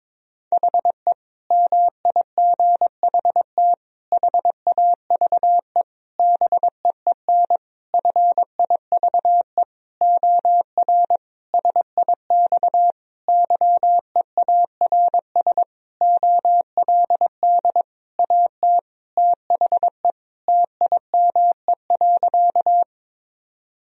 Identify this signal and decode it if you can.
Morse code, heard as a keyed tone, about 22 wpm, 700 Hz. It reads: HE MIGHT HAVE BEEN FIVE OR SIX YEARS OLD AT THE TIME.
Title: Sentences from Top 1000 Words - Review Part 2 - 22wpm